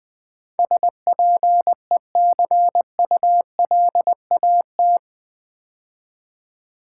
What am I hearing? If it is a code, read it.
Morse code, heard as a keyed tone, about 20 wpm, 700 Hz. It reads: SPECULAT
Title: Build A Word - 4 Letters At A Time - 11-Letter Words - 20wpm